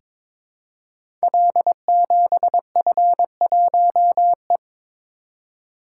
Morse code, heard as a keyed tone, about 23 wpm, 700 Hz. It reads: L7F1E
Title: Code Groups - 5 Letter:Numbers - 22wpm